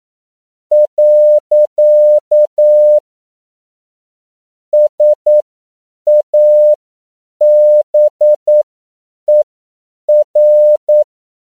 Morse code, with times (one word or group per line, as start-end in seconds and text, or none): 0.7-3.0 .
4.7-11.0 SABER